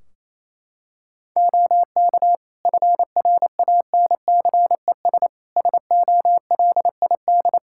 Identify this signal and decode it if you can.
Morse code, heard as a keyed tone, about 28 wpm, 700 Hz. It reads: OK FRANCES SOLID
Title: Practice QSO - Exchange 3 - More Info - 28wpm